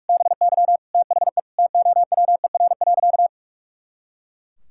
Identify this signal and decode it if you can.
Morse code, heard as a keyed tone, about 41 wpm, 700 Hz. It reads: BY THE TOWER.
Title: Pangram Challenge v2 - 45wpm